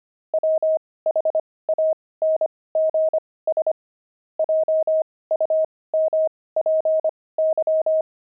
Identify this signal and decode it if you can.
Morse code, heard as a keyed tone, about 25 wpm, spaced as for 18 wpm, 630 Hz. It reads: WHANGS JUMPY
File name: pangram-fs-18-cs-25-veldt